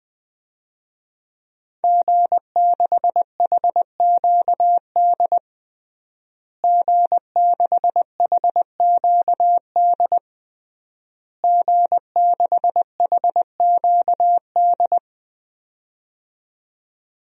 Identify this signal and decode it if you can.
Morse code, heard as a keyed tone, about 20 wpm, 700 Hz. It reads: G6HQD G6HQD G6HQD
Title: DXCC-100 Call Signs - 5 Characters - Repeated 3x - 20wpm